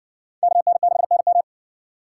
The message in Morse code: KN6NM

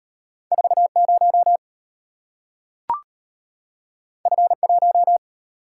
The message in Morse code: 40 F1

